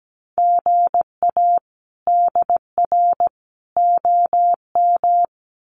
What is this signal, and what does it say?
Morse code, heard as a keyed tone, about 17 wpm, 700 Hz. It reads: GA DR OM
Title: Common QSO Phrases 17wpm